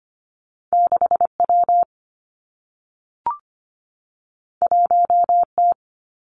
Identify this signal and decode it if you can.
Morse code, heard as a keyed tone, about 25 wpm, 700 Hz. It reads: 6W 1T
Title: Two Challenging Letter-Number Combinations 25wpm